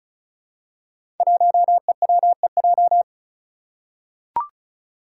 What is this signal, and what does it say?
Morse code, heard as a keyed tone, about 34 wpm, 700 Hz. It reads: 1EWEJ
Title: Five Challenging Letter-Number Combinations 35wpm